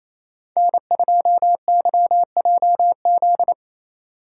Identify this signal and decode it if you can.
Morse code, heard as a keyed tone, about 34 wpm, 700 Hz. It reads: N2YJZ
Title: Speed Racing - USA 1x3 Call Signs 28wpm